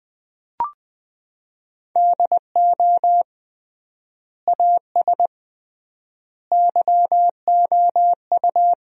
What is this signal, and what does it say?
Morse code, heard as a keyed tone, about 20 wpm, 700 Hz. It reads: DO AS YOU
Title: 3x Word-Spacing - Sentences from Top 100 Words - 20wpm